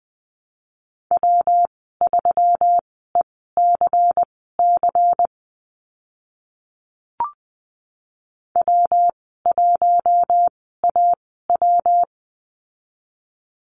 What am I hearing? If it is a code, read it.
Morse code, heard as a keyed tone, about 20 wpm, spaced as for 14 wpm, 700 Hz. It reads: W3ECC W1AW
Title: 122 Lesson 20g - Number 3 - Call Signs 20wpm